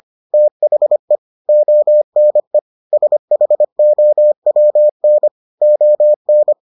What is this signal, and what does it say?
Morse code, heard as a keyed tone, about 25 wpm, 600 Hz. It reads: THE ONE SHOWN ON